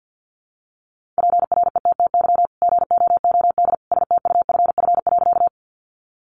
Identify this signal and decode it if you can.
Morse code, heard as a keyed tone, about 50 wpm, 700 Hz. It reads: PRETTY GOOD STUFF.
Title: Sentences from Top 1400 Words 50wpm